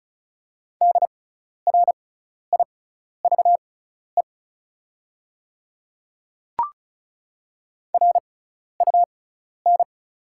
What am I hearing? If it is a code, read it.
Morse code, heard as a keyed tone, about 36 wpm, spaced as for 12 wpm, 700 Hz. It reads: DRIVE RUN
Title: 6x Character Spacing - Top 500 Words - 35wpm